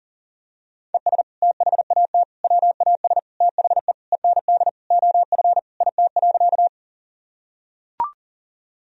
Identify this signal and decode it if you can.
Morse code, heard as a keyed tone, about 40 wpm, 700 Hz. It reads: ES THAT WAS THE END OF IT.
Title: Sentences from Top 200 Words and Abbreviations 40wpm